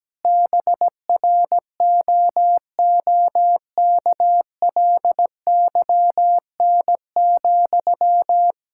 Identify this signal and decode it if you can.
Morse code, heard as a keyed tone, about 17 wpm, 700 Hz. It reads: BROOKLYN,
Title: Practice QSO - Exchange 1 - Beginning - 17wpm